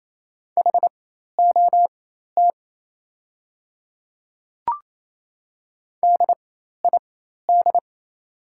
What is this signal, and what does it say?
Morse code, heard as a keyed tone, about 28 wpm, spaced as for 13 wpm, 690 Hz. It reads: HOT DID
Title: 4x Character Spacing - Top 100 Words - 28wpm